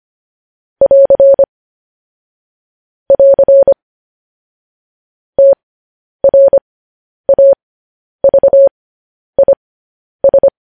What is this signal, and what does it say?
Morse code, heard as a keyed tone, about 25 wpm, spaced as for 10 wpm, 550 Hz. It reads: <AR> <AR> TRAVIS
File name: podcast / News headlines-1x-25x10